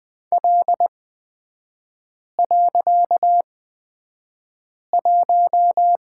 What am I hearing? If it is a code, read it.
Morse code, heard as a keyed tone, about 20 wpm, spaced as for 5 wpm, 700 Hz. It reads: L.1
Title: 255 Lesson 41e - Period - Three Characters 20wpm